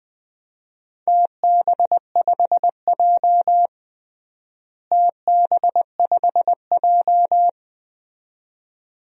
Morse code, heard as a keyed tone, about 21 wpm, 700 Hz. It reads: TB5J TB5J
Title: DXCC-100 Call Signs - 4 Characters - Repeated 2x - 20wpm